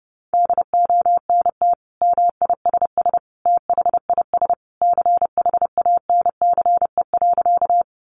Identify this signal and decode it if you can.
Morse code, heard as a keyed tone, about 30 wpm, 700 Hz. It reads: DONT MISS THIS CHANCE.
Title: Sentences from Top 800 Words 30wpm